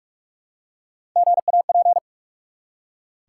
Morse code, heard as a keyed tone, about 44 wpm, 700 Hz. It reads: GAP
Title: Three Letter Words - 45wpm